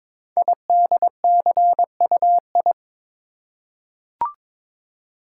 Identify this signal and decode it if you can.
Morse code, heard as a keyed tone, about 22 wpm, 700 Hz. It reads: IDCUI
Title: Code Groups - 5 Letter:Numbers - 22wpm